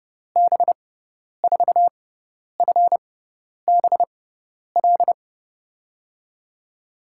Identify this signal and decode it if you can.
Morse code, heard as a keyed tone, about 30 wpm, spaced as for 10 wpm, 700 Hz. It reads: B4FBL
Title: USA Call Signs - 6 Characters - 6x Character Spacing - 30wpm